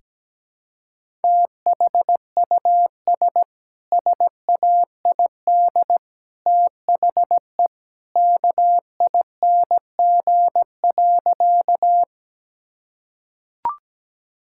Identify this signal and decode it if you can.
Morse code, heard as a keyed tone, about 17 wpm, 700 Hz. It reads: THUS SAID THE KING.
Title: Sentences from Top 1000 Words - Review Part 5 - 17wpm